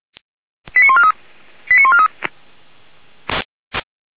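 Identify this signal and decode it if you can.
Five-tone selective call (ZVEI), heard as quick runs of five tones, tones of about 70 ms. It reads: 89143 89143
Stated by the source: Five-tone ZVEI selective call received on SDR.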